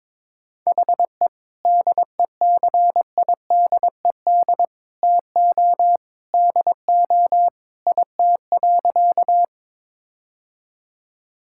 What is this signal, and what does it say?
Morse code, heard as a keyed tone, about 22 wpm, 700 Hz. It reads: HE DECIDED TO DO IT.
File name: practice / 22wpm / Sentences with 3-Word Phrases - 22wpm